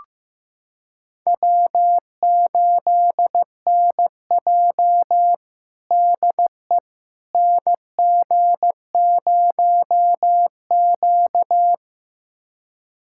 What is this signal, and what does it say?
Morse code, heard as a keyed tone, about 15 wpm, 700 Hz. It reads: W8NJ DE NG0Q
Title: USA Callsign DE Callsign - 4 Characters - 15wpm